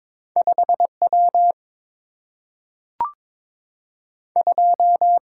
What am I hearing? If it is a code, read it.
Morse code, heard as a keyed tone, about 22 wpm, 700 Hz. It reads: 5 W 2
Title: Single-Letter Single-Number 22wpm